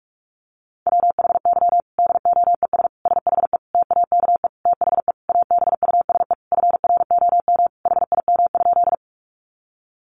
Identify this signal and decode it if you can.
Morse code, heard as a keyed tone, about 44 wpm, 700 Hz. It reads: WHY DOES SHE TAKE THE ABUSE FROM HIM?
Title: Sentences from Top 1300 Words 45wpm